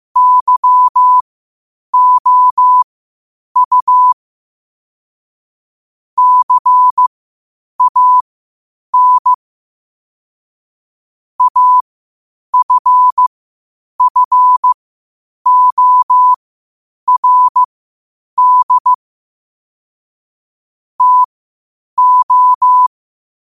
Morse code, heard as a keyed tone, about 15 wpm, spaced as for 8 wpm, 1000 Hz. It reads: YOU CAN AFFORD TO
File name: qotdfiles / 8wpm